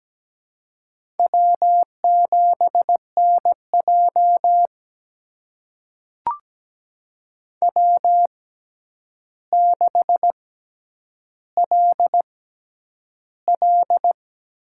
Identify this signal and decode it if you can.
Morse code, heard as a keyed tone, about 17 wpm, spaced as for 9 wpm, 700 Hz. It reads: W7NJ W6LL
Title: USA Call Signs - 6x Character Spacing - 17wpm